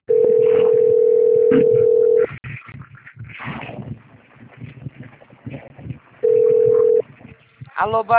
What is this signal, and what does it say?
Ringback tone (North America), 440 + 480 Hz, with speech.
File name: out- - - - - .97862